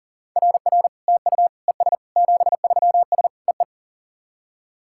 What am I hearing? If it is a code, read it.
Morse code, heard as a keyed tone, about 40 wpm, 690 Hz. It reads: RR TU ES 73S EE
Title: SOTA QSO Practice 40wpm